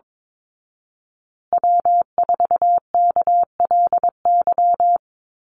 Morse code, heard as a keyed tone, about 22 wpm, 700 Hz. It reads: W4KLY
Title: Call Signs Repeated 2x - 22wpm